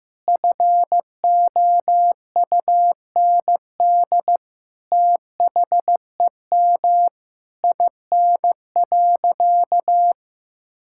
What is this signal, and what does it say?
Morse code, heard as a keyed tone, about 16 wpm, 700 Hz. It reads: FOUND THEM IN.
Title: Sentences from Top 1000 Words - Review Part 6 - 15wpm